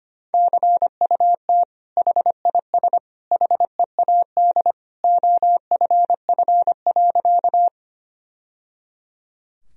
Morse code, heard as a keyed tone, about 25 wpm, 700 Hz. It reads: CUT HIS HEAD OFF.
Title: Sentences from Top 1000 Words - Review Part 1 - 25wpm